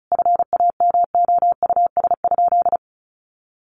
Morse code, heard as a keyed tone, about 34 wpm, 700 Hz. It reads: FAMOUS?
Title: Sentences from Top 1500 Words 35wpm